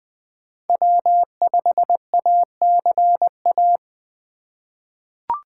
Morse code, heard as a keyed tone, about 20 wpm, 700 Hz. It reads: W5ACA
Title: USA Call Signs - 4x Character Spacing - 20wpm